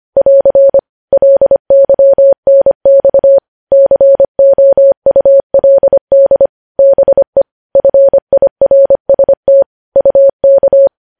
Morse code, heard as a keyed tone, about 25 wpm, 550 Hz. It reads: <AR> LYNX COULD BE FIRST UK
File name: podcast / News headlines-1x-25x25